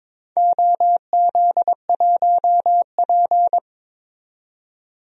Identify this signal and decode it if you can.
Morse code, heard as a keyed tone, about 22 wpm, 700 Hz. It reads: OZ1P
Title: DXCC-100 Call Signs - 4 Characters - 22wpm